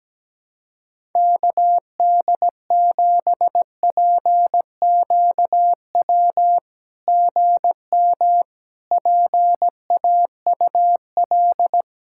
Morse code, heard as a keyed tone, about 17 wpm, 700 Hz. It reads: KD7PQW GM PAUL
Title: SOTA QSO Practice 17wpm